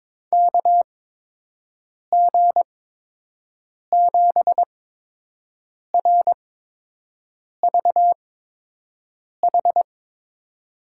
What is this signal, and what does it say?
Morse code, heard as a keyed tone, about 22 wpm, spaced as for 6 wpm, 700 Hz. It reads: KG7RVH
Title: USA Call Signs - 6 Characters - 8x Character Spacing - 22wpm